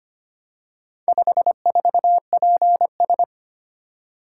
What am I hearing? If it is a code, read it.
Morse code, heard as a keyed tone, about 25 wpm, 700 Hz. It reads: 54PS
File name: practice / 25wpm / Four Letter-Number Combinations - 25wpm